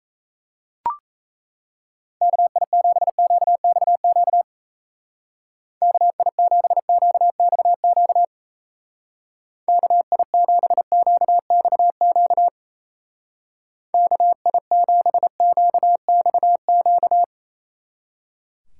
Morse code, heard as a keyed tone, about 35 wpm, 700 Hz. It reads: KI7QXQ KI7QXQ KI7QXQ KI7QXQ